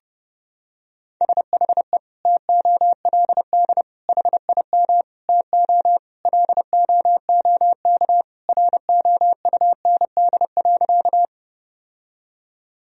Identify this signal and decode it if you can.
Morse code, heard as a keyed tone, about 32 wpm, 700 Hz. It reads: SHE TOLD HIM TO LOOK ROUND.
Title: Sentences from Top 1000 Words - Review Part 1 - 30wpm